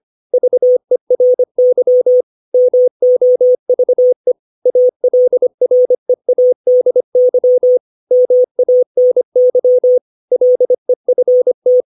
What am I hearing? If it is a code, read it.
Morse code, heard as a keyed tone, about 25 wpm, 500 Hz. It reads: VERY MOVE ALREADY MANY LEFT
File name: 632 words / 632 Words 25WPM